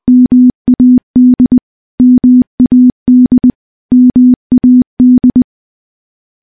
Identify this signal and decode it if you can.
Morse code, heard as a keyed tone, about 20 wpm, 250 Hz. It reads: MAD MAD MAD
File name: morse / morse-training-20WPM